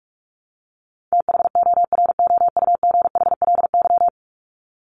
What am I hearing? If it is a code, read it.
Morse code, heard as a keyed tone, about 44 wpm, 700 Hz. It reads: THOROUGHLY
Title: LY Suffix - Top 100 Words with Base Words 45wpm